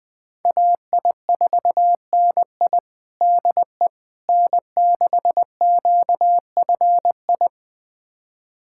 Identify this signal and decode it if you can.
Morse code, heard as a keyed tone, about 20 wpm, 700 Hz. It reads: AI4NI DE N6QFI
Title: USA Callsign DE Callsign - 5 Characters - 20wpm